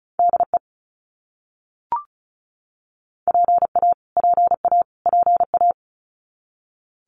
Morse code, heard as a keyed tone, about 34 wpm, 700 Hz. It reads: DE PA PA PA
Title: US States Repeated 3x 35wpm